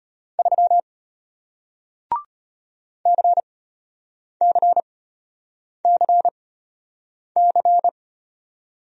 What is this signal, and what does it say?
Morse code, heard as a keyed tone, about 34 wpm, spaced as for 8 wpm, 700 Hz. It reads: 3 CCCC